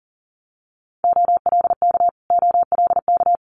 Morse code, heard as a keyed tone, about 40 wpm, 700 Hz. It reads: OLK OLK
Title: Multi-letter Phonemes Repeated 3x 40wpm